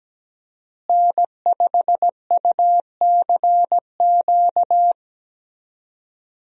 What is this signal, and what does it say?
Morse code, heard as a keyed tone, about 17 wpm, 700 Hz. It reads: N5UCQ
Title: USA Call Signs - 5 Characters - Repeated 3x - 17wpm